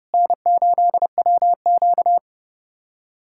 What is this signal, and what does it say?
Morse code, heard as a keyed tone, about 30 wpm, 700 Hz. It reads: N8WQ
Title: USA Callsign DE Callsign 30wpm